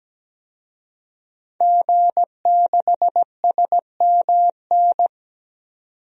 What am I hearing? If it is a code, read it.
Morse code, heard as a keyed tone, about 17 wpm, 700 Hz. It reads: G6SMN
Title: DXCC-100 Call Signs - 5 Characters - 17wpm